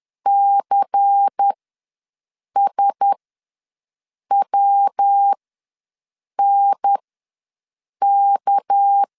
Morse code, heard as a keyed tone, about 10 wpm, spaced as for 6 wpm, 800 Hz. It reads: CSWNK
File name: LEZIONI / 6 PKX0 / 6 PK ws 10wpm 3745